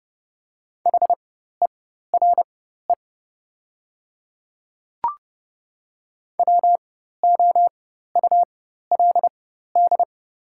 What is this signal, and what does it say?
Morse code, heard as a keyed tone, about 30 wpm, spaced as for 14 wpm, 700 Hz. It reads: HERE WOULD
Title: 4x Character Spacing - Top 200 Words - 30wpm